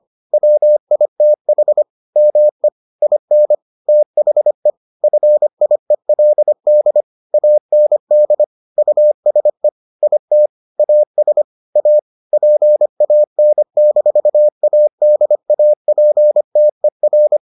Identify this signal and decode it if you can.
Morse code, heard as a keyed tone, about 25 wpm, 600 Hz. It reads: WITH ME IN THE FIELD AND USE IT AS A PAN-ADAPTER